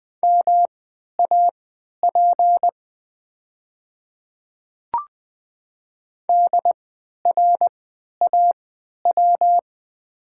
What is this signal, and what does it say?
Morse code, heard as a keyed tone, about 20 wpm, spaced as for 11 wpm, 700 Hz. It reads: MAP DRAW
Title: 3x Character Spacing - Top 500 Words - 20wpm